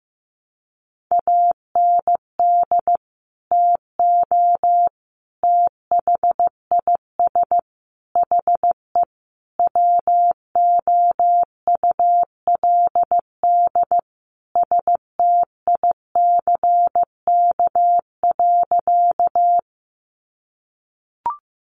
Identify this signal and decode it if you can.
Morse code, heard as a keyed tone, about 15 wpm, 700 Hz. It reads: AND TO THIS HE WOULD STICK.
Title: Sentences from Top 1200 Words 15wpm